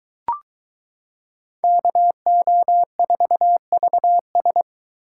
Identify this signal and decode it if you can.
Morse code, heard as a keyed tone, about 22 wpm, 700 Hz. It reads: KO4VS